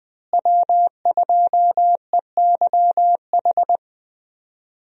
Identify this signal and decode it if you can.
Morse code, heard as a keyed tone, about 20 wpm, 700 Hz. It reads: W2EYH
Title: USA Call Signs - 5 Characters - Repeated 2x - 20wpm